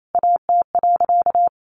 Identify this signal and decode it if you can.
Morse code, heard as a keyed tone, about 28 wpm, 700 Hz. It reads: AT.
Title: Sentences from Top 1100 Words 28wpm